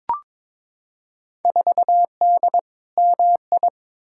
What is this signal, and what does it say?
Morse code, heard as a keyed tone, about 22 wpm, 700 Hz. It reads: 4D MI
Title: Field Day Exchanges Encore - 22wpm